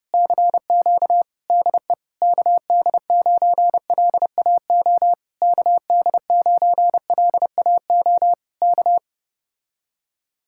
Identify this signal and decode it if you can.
Morse code, heard as a keyed tone, about 30 wpm, 700 Hz. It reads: CQ DE KD9LAO KD9LAO K
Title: Speed-Racing Common QSO Phrases Encore - 25wpm